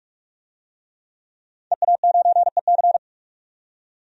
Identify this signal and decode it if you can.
Morse code, heard as a keyed tone, about 46 wpm, 700 Hz. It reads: EA9EC
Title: DXCC-100 Call Signs - 5 Characters - 45wpm